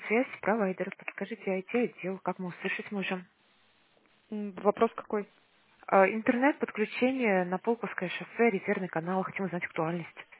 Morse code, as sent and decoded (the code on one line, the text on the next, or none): none